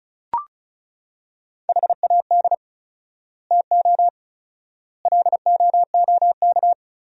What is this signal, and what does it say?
Morse code, heard as a keyed tone, about 36 wpm, 700 Hz. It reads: HAD TO LOOK
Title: 4x Word-Spacing - Sentences from Top 100 Words - 35wpm